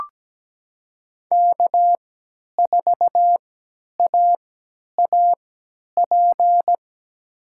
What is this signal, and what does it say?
Morse code, heard as a keyed tone, about 17 wpm, spaced as for 10 wpm, 700 Hz. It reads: K4AAP
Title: USA Call Signs - 5 Characters - 3x Character Spacing - 17wpm